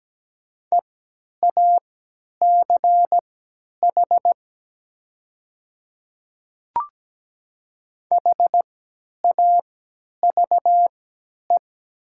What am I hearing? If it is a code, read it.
Morse code, heard as a keyed tone, about 17 wpm, spaced as for 10 wpm, 700 Hz. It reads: EACH HAVE
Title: 3x Character Spacing - Top 100 Words - 17wpm